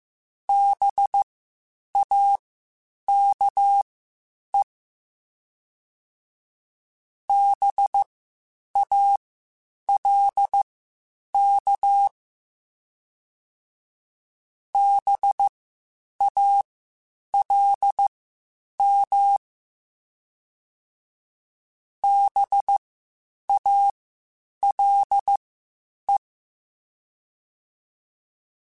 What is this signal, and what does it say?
Morse code, heard as a keyed tone, about 15 wpm, spaced as for 8 wpm, 800 Hz. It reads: BAKE BALK BALM BALE
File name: Spaced Alphas / Spaced Alphas mp3s / Bs sp @13